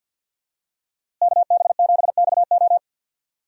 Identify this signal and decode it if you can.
Morse code, heard as a keyed tone, about 46 wpm, 700 Hz. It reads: KB7XO
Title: USA Call Signs - 5 Characters - Repeated 3x - 50wpm